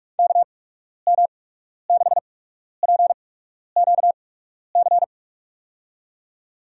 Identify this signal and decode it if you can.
Morse code, heard as a keyed tone, about 43 wpm, spaced as for 12 wpm, 700 Hz. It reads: KM6PQC
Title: USA Call Signs - 6 Characters - 8x Character Spacing - 45wpm